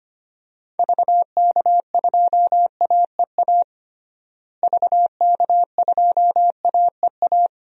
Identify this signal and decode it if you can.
Morse code, heard as a keyed tone, about 25 wpm, 700 Hz. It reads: VK2AEA VK2AEA
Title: DXCC-100 Call Signs - 6 Characters - Repeated 2x - 25wpm